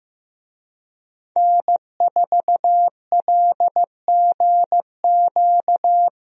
Morse code, heard as a keyed tone, about 15 wpm, 700 Hz. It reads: N4LGQ
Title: USA Call Signs - 5 Characters - Repeated 2x - 15wpm